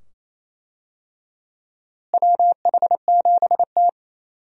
Morse code, heard as a keyed tone, about 28 wpm, 700 Hz. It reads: WH7T
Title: USA Call Signs - 4 Characters - Repeated 2x - 28wpm